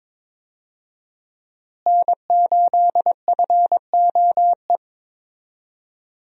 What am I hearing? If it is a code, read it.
Morse code, heard as a keyed tone, about 22 wpm, 700 Hz. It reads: N8FOE
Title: USA Call Signs - 5 Characters - Repeated 2x - 22wpm